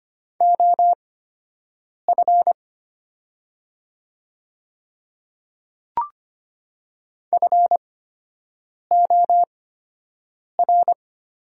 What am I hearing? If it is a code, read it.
Morse code, heard as a keyed tone, about 25 wpm, spaced as for 7 wpm, 700 Hz. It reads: OF FOR
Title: 8x Character Spacing - Top 100 Words - 25wpm